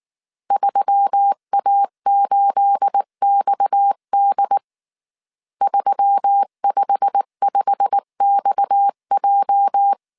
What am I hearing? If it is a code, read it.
Morse code, heard as a keyed tone, about 19 wpm, 800 Hz. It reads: 3A8XD 355XJ